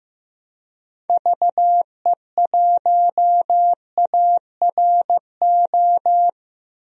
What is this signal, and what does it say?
Morse code, heard as a keyed tone, about 15 wpm, 700 Hz. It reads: VE1ARO
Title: DXCC-100 Call Signs - 6 Characters - 15wpm